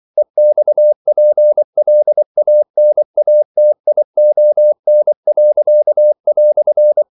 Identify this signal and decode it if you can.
Morse code, heard as a keyed tone, about 24 wpm, 600 Hz. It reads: EXPLANATION."
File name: CW / Practice / EnglishQuotations0224